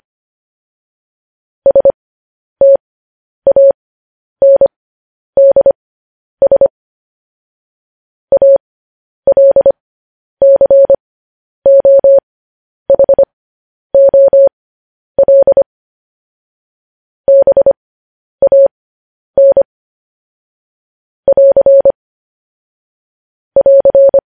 Morse code, heard as a keyed tone, about 25 wpm, spaced as for 10 wpm, 550 Hz. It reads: STANDS ALCOHOL BAN <AR> <AR>
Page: news-headlines-in-morse-code-at-10-wpm